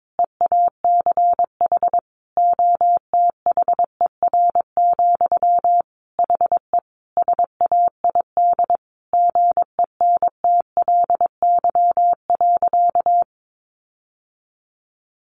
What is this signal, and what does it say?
Morse code, heard as a keyed tone, about 22 wpm, 700 Hz. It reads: EACH OTHER, HE SAID GENTLY.
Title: Sentences from Top 1500 Words 22wpm